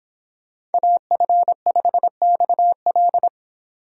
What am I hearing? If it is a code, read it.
Morse code, heard as a keyed tone, about 31 wpm, 700 Hz. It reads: AF5XL